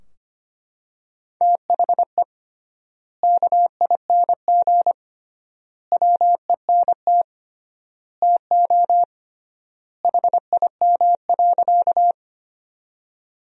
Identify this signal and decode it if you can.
Morse code, heard as a keyed tone, about 25 wpm, 700 Hz. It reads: THE KING WENT TO HIM.
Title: 3x Word-Spacing - Sentences from Top 400 Words - 25wpm